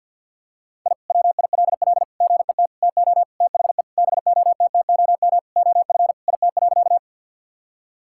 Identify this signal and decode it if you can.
Morse code, heard as a keyed tone, about 51 wpm, 700 Hz. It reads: I WILL GET TO THE BOTTOM OF IT.
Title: Sentences from Top 1000 Words - Review Part 6 - 50wpm